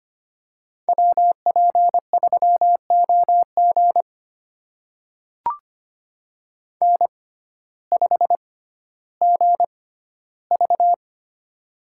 Morse code, heard as a keyed tone, about 25 wpm, spaced as for 14 wpm, 700 Hz. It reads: WP3OG N5GV
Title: USA Call Signs - 6x Character Spacing - 25wpm